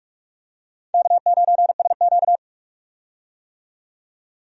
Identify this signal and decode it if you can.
Morse code, heard as a keyed tone, about 44 wpm, 700 Hz. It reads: K9SQ
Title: USA Call Signs - 4 Characters - 45wpm